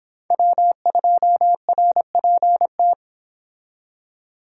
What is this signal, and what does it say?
Morse code, heard as a keyed tone, about 31 wpm, 700 Hz. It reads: W2RPT